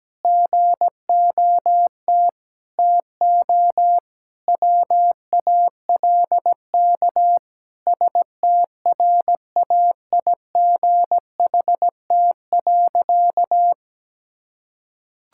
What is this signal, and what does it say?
Morse code, heard as a keyed tone, about 17 wpm, 700 Hz. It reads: GOT TO WALK STRAIGHT.
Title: Sentences from Top 1000 Words - Review Part 6 - 17wpm